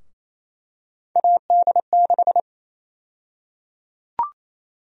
Morse code, heard as a keyed tone, about 28 wpm, 700 Hz. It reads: AD6
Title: USA Call Sign Prefixes - Repeated 2x - 28wpm